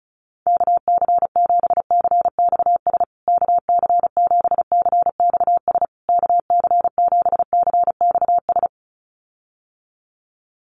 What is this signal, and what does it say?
Morse code, heard as a keyed tone, about 34 wpm, 700 Hz. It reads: KC7CXS KC7CXS KC7CXS
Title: USA Call Signs Repeated 3x 35wpm